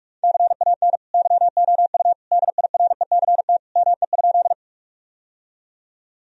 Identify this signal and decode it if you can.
Morse code, heard as a keyed tone, about 41 wpm, 700 Hz. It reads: CAN YOU DIRECT ME?
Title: Sentences from Top 1000 Words - Review Part 2 - 45wpm